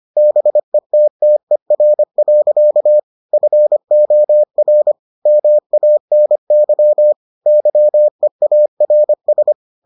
Morse code, heard as a keyed tone, about 25 wpm, 600 Hz. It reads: BETTER. FOR MANY YEARS